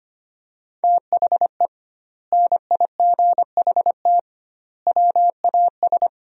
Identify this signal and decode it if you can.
Morse code, heard as a keyed tone, about 25 wpm, 700 Hz. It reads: THE NIGHT WAS
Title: 2x Word-Spacing - Sentences from Top 400 Words - 25wpm